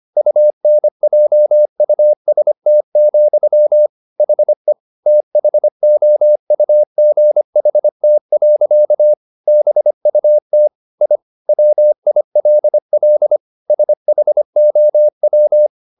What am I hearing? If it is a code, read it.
Morse code, heard as a keyed tone, about 25 wpm, 600 Hz. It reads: UNJUST, HE THOUGHT. BUT I WILL SHOW
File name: the old man and the sea part 1 25wpm